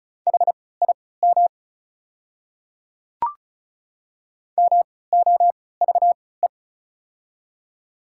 Morse code, heard as a keyed tone, about 36 wpm, spaced as for 20 wpm, 700 Hz. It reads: HIM MOVE
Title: 3x Character Spacing - Top 200 Words - 35wpm